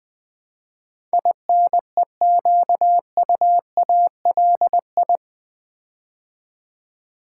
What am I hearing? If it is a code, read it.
Morse code, heard as a keyed tone, about 21 wpm, 700 Hz. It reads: INEQUALI
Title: Build A Word - 4 Letters At A Time - 10-Letter Words - 20wpm